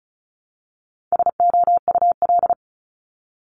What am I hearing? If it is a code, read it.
Morse code, heard as a keyed tone, about 36 wpm, 700 Hz. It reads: SOUL
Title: Multi-letter Phonemes Repeated 3x 35wpm